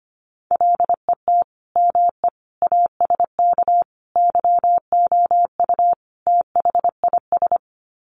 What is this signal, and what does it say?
Morse code, heard as a keyed tone, about 25 wpm, 700 Hz. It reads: LET ME ASK YOU THIS
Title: Sets of 5 Words 25wpm